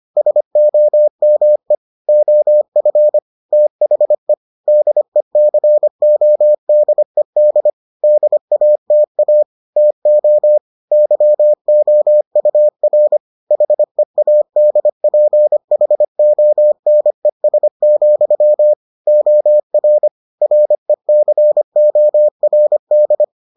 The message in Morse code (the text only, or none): SOME OF THE DECODED DATA TO YOUR HEADPHONES, OR RECORD